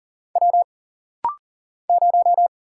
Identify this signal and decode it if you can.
Morse code, heard as a keyed tone, about 40 wpm, 700 Hz. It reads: W 0
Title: Single Letter-Number Alt - Rapid-Fire 40wpm